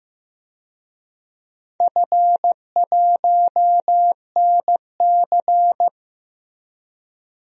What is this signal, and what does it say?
Morse code, heard as a keyed tone, about 15 wpm, 700 Hz. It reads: F1NC
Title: DXCC-100 Call Signs - 4 Characters - Repeated 2x - 15wpm